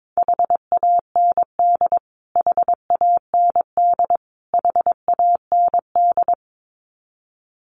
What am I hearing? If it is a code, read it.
Morse code, heard as a keyed tone, about 22 wpm, 700 Hz. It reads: HAND HAND HAND
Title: Top 500 Words Repeated 22wpm